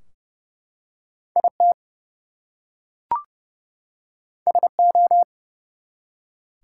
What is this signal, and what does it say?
Morse code, heard as a keyed tone, about 31 wpm, 700 Hz. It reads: IT SO
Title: Two Letter Words - 30wpm